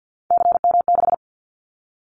Morse code, heard as a keyed tone, about 49 wpm, 700 Hz. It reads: CM6
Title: Callsign Prefixes for Top 100 DXCC - 50wpm